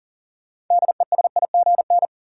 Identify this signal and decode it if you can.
Morse code, heard as a keyed tone, about 37 wpm, 700 Hz. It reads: DESIGN
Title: Sentences from Top 800 Words - New Word Drill - 40wpm